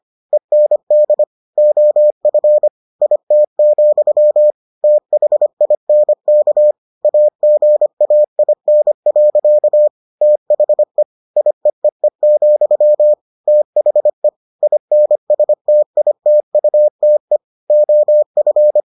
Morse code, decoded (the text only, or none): END OF IT, THINK AGAIN. THE IEEE, THE INSTITUTE OF